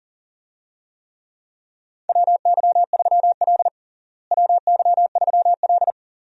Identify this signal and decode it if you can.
Morse code, heard as a keyed tone, about 40 wpm, 700 Hz. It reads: WY3L WY3L
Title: USA Call Signs - 4 Characters - Repeated 3x - 40wpm